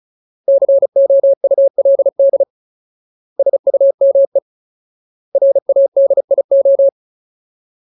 Morse code, heard as a keyed tone, about 34 wpm, 550 Hz. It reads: COULD SUME RADIO
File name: Ultimate Word List / 35WPM Ultimate